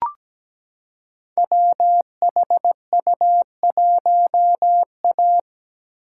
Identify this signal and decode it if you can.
Morse code, heard as a keyed tone, about 17 wpm, 700 Hz. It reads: WHU1A
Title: Code Groups - 5 Letter:Numbers - 17wpm